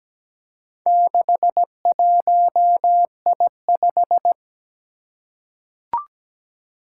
Morse code, heard as a keyed tone, about 17 wpm, 700 Hz. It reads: 61I5
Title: Four Challenging Letter-Number Combinations 17wpm